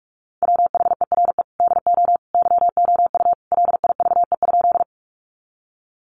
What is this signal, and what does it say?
Morse code, heard as a keyed tone, about 44 wpm, 700 Hz. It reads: WHERE DO YOU LIVE?
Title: Sentences from Top 200 Words 45wpm